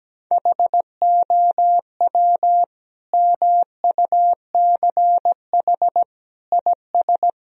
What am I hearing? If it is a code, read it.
Morse code, heard as a keyed tone, about 17 wpm, 700 Hz. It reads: HOW MUCH IS
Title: Sentences from Top 1000 Words - Review Part 2 - 17wpm